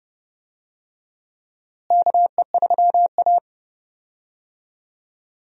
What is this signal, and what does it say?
Morse code, heard as a keyed tone, about 30 wpm, 700 Hz. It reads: KE3A
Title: USA Call Signs - 4 Characters - Repeated 2x - 30wpm